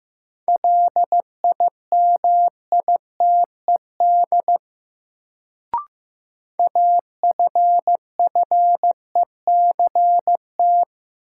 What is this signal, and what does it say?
Morse code, heard as a keyed tone, about 15 wpm, 700 Hz. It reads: LIMITED AFFECT
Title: Sentences from Top 900 Words - New Word Drill - 15wpm